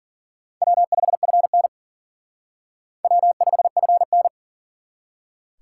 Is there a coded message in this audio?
Yes: Morse code from 0.6 to 4.3 s, heard as a keyed tone, about 52 wpm, 700 Hz. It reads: W5FN W5FN